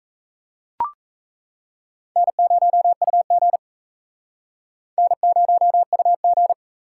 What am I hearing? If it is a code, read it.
Morse code, heard as a keyed tone, about 34 wpm, 700 Hz. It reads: N0UG N0UG
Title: Speed Racing - USA 1x2 Call Signs 28wpm